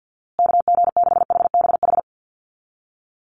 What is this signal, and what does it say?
Morse code, heard as a keyed tone, about 50 wpm, 700 Hz. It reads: KG6HBH
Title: USA Call Signs Repeated 2x 50wpm